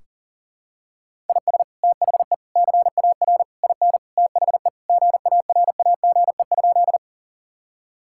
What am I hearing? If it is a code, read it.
Morse code, heard as a keyed tone, about 40 wpm, 700 Hz. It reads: IS THE CAR IN THE GARAGE?
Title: Sentences from Top 1600 Words 40wpm